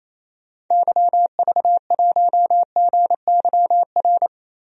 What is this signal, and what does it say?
Morse code, heard as a keyed tone, about 28 wpm, 700 Hz. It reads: YV1GYR
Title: DXCC-100 Call Signs - 6 Characters - Repeated 2x - 28wpm